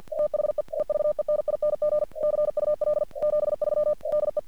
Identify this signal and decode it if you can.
Morse code, heard as a keyed tone, about 54 wpm, 600 Hz. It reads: THE EVENING CUL 73 DE
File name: archived / CRSnet